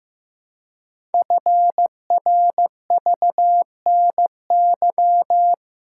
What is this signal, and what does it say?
Morse code, heard as a keyed tone, about 15 wpm, 700 Hz. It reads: FRVNY
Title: Code Groups - 5 Letters - 15wpm